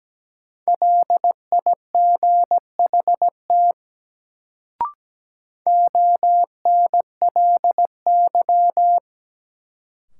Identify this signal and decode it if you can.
Morse code, heard as a keyed tone, about 17 wpm, 700 Hz. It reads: LIGHT ONLY
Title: Sentences from Top 200 Words - New Word Drill - 17wpm